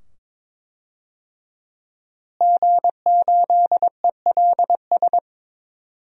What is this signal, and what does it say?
Morse code, heard as a keyed tone, about 22 wpm, 700 Hz. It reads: G8ELS
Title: DXCC-100 Call Signs - 5 Characters - Repeated 2x - 22wpm